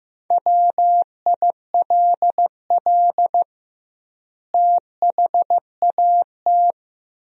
Morse code, heard as a keyed tone, about 15 wpm, 700 Hz. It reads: WILL THAT
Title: 2x Word-Spacing - Sentences from Top 100 Words - 15wpm